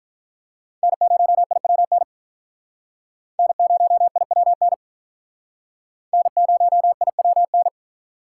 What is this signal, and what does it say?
Morse code, heard as a keyed tone, about 45 wpm, 700 Hz. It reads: N0IWN N0IWN N0IWN